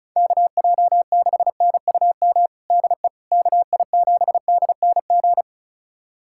Morse code, heard as a keyed tone, about 34 wpm, 700 Hz. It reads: KJ6NUM DE KI7DNG
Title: USA Callsign DE Callsign - 6 Characters - 35wpm